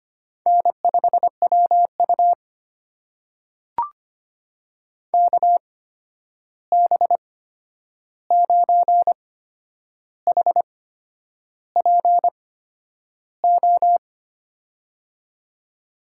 Morse code, heard as a keyed tone, about 25 wpm, spaced as for 9 wpm, 700 Hz. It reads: N5WU KB9HPO
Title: USA Call Signs - 8x Character Spacing - 25wpm